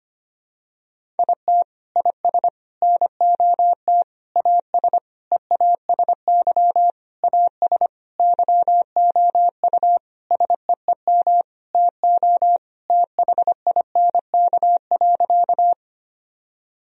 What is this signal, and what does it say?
Morse code, heard as a keyed tone, about 25 wpm, 700 Hz. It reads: IT IS NOT AS EASY AS YOU SEEM TO THINK.
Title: Sentences from Top 1000 Words - Review Part 1 - 25wpm